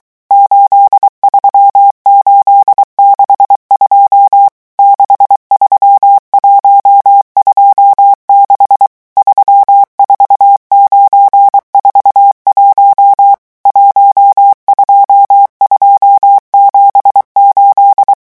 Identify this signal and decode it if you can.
Morse code, heard as a keyed tone, about 23 wpm, 800 Hz. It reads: 83862 63126 34941 12278
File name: morse mp3 / revision11 chiffres vitesse 22